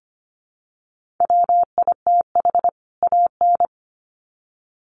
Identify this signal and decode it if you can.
Morse code, heard as a keyed tone, about 25 wpm, 700 Hz. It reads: WITH AN
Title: Sets of 2 Words 25wpm